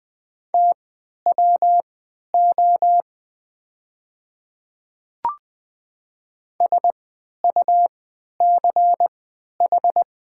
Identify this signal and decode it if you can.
Morse code, heard as a keyed tone, about 20 wpm, spaced as for 11 wpm, 700 Hz. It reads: TWO SUCH
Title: 3x Character Spacing - Top 200 Words - 20wpm